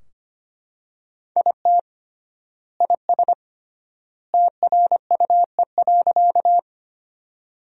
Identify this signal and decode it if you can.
Morse code, heard as a keyed tone, about 25 wpm, 700 Hz. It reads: IT IS TRUE.
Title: 3x Word-Spacing - Sentences from Top 400 Words - 25wpm